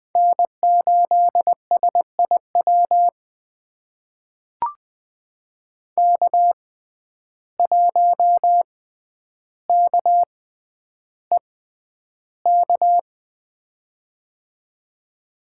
Morse code, heard as a keyed tone, about 20 wpm, spaced as for 10 wpm, 700 Hz. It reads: N8SIW K1KEK
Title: USA Call Signs - 5 Characters - 6x Character Spacing - 20wpm